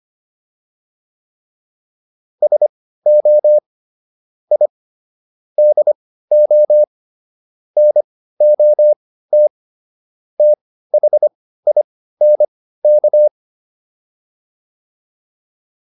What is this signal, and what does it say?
Morse code, heard as a keyed tone, about 25 wpm, spaced as for 15 wpm, 600 Hz. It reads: SO I DO NOT THINK
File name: WD405 15